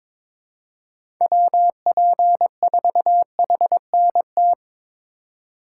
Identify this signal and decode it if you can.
Morse code, heard as a keyed tone, about 23 wpm, 700 Hz. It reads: WP4HNT
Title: USA Call Signs - 6 Characters - 3x Character Spacing - 22wpm